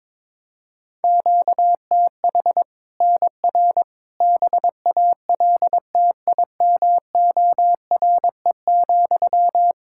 Morse code, heard as a keyed tone, about 22 wpm, 700 Hz. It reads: QTH NR BALTIMORE,
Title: Practice QSO - Exchange 1 - Beginning - 22wpm